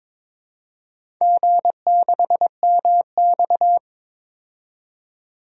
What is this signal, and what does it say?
Morse code, heard as a keyed tone, about 22 wpm, 700 Hz. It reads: G6MX
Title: DXCC-100 Call Signs - 4 Characters - 22wpm